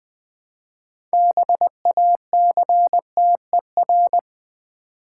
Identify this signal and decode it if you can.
Morse code, heard as a keyed tone, about 20 wpm, 700 Hz. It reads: BACTER
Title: Build A Word - 3 Letters At A Time - 8-Letter Words - 20wpm